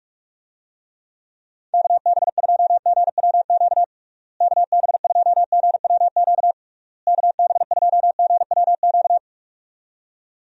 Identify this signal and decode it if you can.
Morse code, heard as a keyed tone, about 46 wpm, 700 Hz. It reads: KB2GWQ KB2GWQ KB2GWQ
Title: USA Call Signs - 6 Characters - Repeated 3x - 45wpm